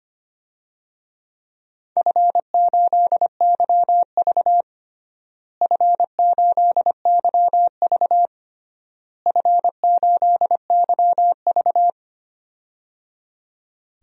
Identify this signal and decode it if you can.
Morse code, heard as a keyed tone, about 25 wpm, 700 Hz. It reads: F8YV F8YV F8YV
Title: DXCC-100 Call Signs - 4 Characters - Repeated 3x - 25wpm